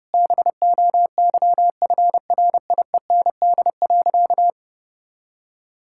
Morse code, heard as a keyed tone, about 30 wpm, 700 Hz. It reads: BOYFRIEND.
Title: Sentences from Top 1000 Words - Review Part 5 - 30wpm